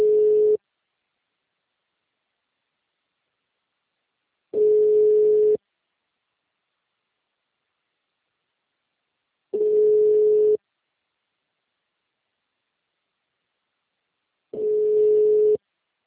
Ringback tone (Europe, 425 Hz).
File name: out- -2030- - - .83951